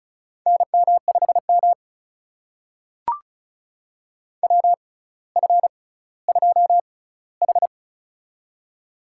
Morse code, heard as a keyed tone, about 36 wpm, spaced as for 18 wpm, 700 Hz. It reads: NM5M WF2H
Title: USA Call Signs - 4 Characters - 6x Character Spacing - 35wpm